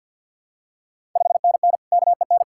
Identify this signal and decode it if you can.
Morse code, heard as a keyed tone, about 53 wpm, 690 Hz. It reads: 5NN KEN
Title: Ontario QSO Party Exchange 50wpm